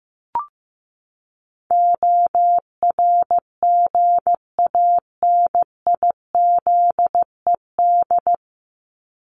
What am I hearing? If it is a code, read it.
Morse code, heard as a keyed tone, about 15 wpm, 700 Hz. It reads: ORGANIZED
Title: Top 1001-1250 Words 15wpm